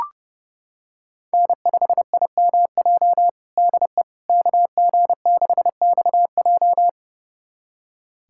Morse code, heard as a keyed tone, about 30 wpm, 700 Hz. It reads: N5IMJ DE KG6XJ
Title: USA Callsign DE Callsign - 5 Characters - 30wpm